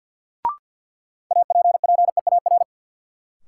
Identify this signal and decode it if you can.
Morse code, heard as a keyed tone, about 51 wpm, 700 Hz. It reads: APPEAR
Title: Sentences from Top 1200 Words - New Word Drill Alt - 50wpm